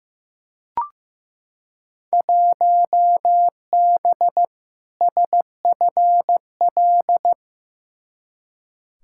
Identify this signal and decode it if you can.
Morse code, heard as a keyed tone, about 15 wpm, 700 Hz. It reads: 1B SFL
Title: Field Day Exchanges Encore - 15wpm